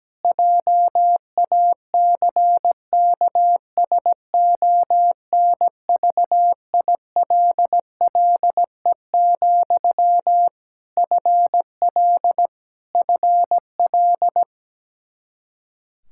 Morse code, heard as a keyed tone, about 17 wpm, 700 Hz. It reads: JACKSONVILLE, FL FL
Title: Practice QSO - Exchange 1 - Beginning - 17wpm